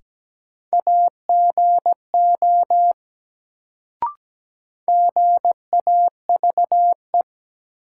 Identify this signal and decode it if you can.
Morse code, heard as a keyed tone, about 17 wpm, 700 Hz. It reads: AGO GAVE
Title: Sentences from Top 500 Words - New Word Drill - 17wpm